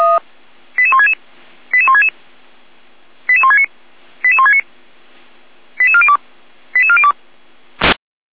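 Five-tone selective call (ZVEI), heard as quick runs of five tones, tones of about 70 ms. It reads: 88166 88166 88169 88169 88402 88402
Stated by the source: ZVEI five-tone selective call 88207 received on an SDR.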